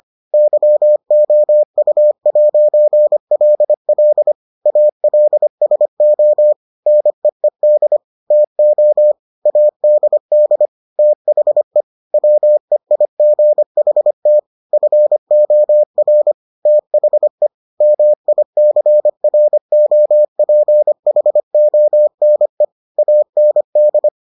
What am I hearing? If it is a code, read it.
Morse code, heard as a keyed tone, about 25 wpm, 600 Hz. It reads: YOU'LL ALSO NEED TO ADD THE WEIGHT FOR THE MICROPHONE AND